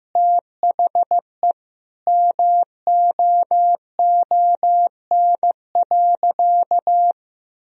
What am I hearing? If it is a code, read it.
Morse code, heard as a keyed tone, about 16 wpm, 700 Hz. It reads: THE MOON.
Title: Sentences from Top 1000 Words - Review Part 3 - 15wpm